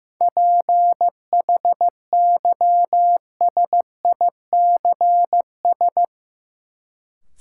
Morse code, heard as a keyed tone, about 16 wpm, 700 Hz. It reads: PHYSICS
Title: Sentences from Top 1000 Words - Review Part 5 - New Word Drill - 15wpm